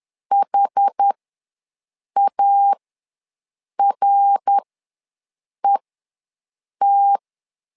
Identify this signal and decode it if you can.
Morse code, heard as a keyed tone, about 10 wpm, spaced as for 6 wpm, 800 Hz. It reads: HARET
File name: LEZIONI / 3 MWRO / 3 MWR ws 10wpm 2151